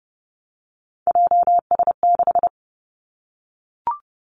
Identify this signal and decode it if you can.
Morse code, heard as a keyed tone, about 30 wpm, 700 Hz. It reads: JS6
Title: Three Challenging Letter-Number Combinations 30wpm